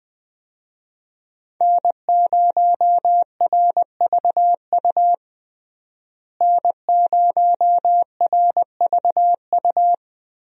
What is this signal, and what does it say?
Morse code, heard as a keyed tone, about 21 wpm, 700 Hz. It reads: N0RVU N0RVU
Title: USA Call Signs - 5 Characters - Repeated 3x - 20wpm